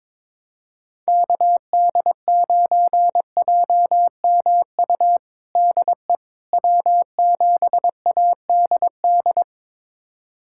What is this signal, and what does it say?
Morse code, heard as a keyed tone, about 21 wpm, 700 Hz. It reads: KD9JMU DE W7ADD
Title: USA Callsign DE Callsign 22wpm